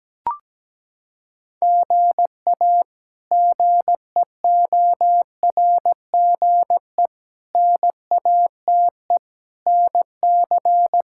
Morse code, heard as a keyed tone, about 17 wpm, 700 Hz. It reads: GA GEORGE NATE NC
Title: K1USN SST QSO Practice - 17wpm